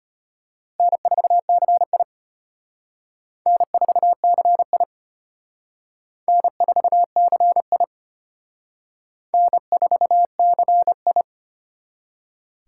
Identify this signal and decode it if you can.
Morse code, heard as a keyed tone, about 31 wpm, 700 Hz. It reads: N4CI N4CI N4CI N4CI
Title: Speed Racing - USA 1x2 Call Signs 25wpm